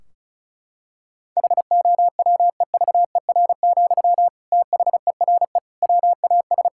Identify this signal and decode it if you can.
Morse code, heard as a keyed tone, about 34 wpm, 700 Hz. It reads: HOWEVER, THERE WAS
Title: Sentences from Top 1600 Words 35wpm